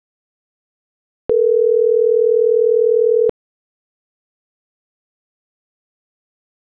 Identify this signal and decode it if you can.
Ringback tone (North America), 440 + 480 Hz.